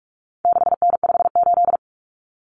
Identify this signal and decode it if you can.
Morse code, heard as a keyed tone, about 44 wpm, 700 Hz. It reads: 6N58
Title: Common Cut Numbers - 4-Digit 45wpm